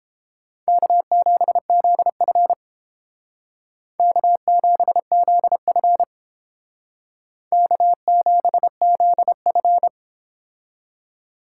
Morse code, heard as a keyed tone, about 29 wpm, 700 Hz. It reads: K7ZF K7ZF K7ZF